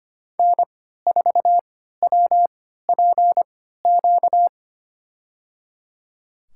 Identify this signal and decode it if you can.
Morse code, heard as a keyed tone, about 25 wpm, spaced as for 14 wpm, 700 Hz. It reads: N4WPQ
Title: USA Call Signs - 5 Characters - 3x Character Spacing - 25wpm